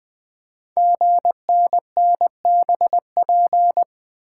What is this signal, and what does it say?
Morse code, heard as a keyed tone, about 20 wpm, 700 Hz. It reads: GNNBP
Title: Code Groups - 5 Letters - 20wpm